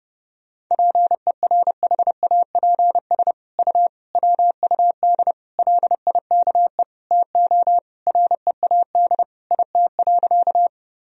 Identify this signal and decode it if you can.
Morse code, heard as a keyed tone, about 30 wpm, 700 Hz. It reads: PERHAPS U WUD LIKE TO READ IT.
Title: Sentences from Top 500 Words and Abbreviations 30wpm